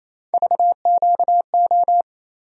Morse code, heard as a keyed tone, about 28 wpm, 700 Hz. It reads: VQO
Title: USA Call Sign Postfixes - 3 Letters 28wpm